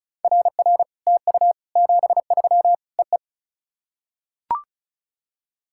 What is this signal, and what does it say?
Morse code, heard as a keyed tone, about 36 wpm, 700 Hz. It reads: RR TU 73 EE
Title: SOTA QSO Practice 35wpm